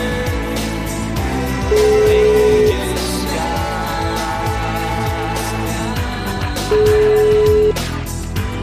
Ringback tone (Europe, 425 Hz), with noise.